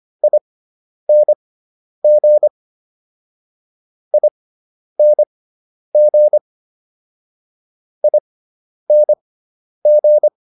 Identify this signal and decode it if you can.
Morse code, heard as a keyed tone, about 25 wpm, spaced as for 10 wpm, 600 Hz. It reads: ING ING ING